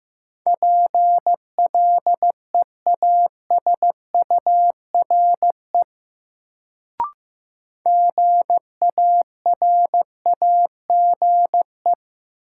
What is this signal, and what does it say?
Morse code, heard as a keyed tone, about 15 wpm, 700 Hz. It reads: PLEASURE GARAGE
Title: Sentences from Top 1600 Words - New Word Drill - 15wpm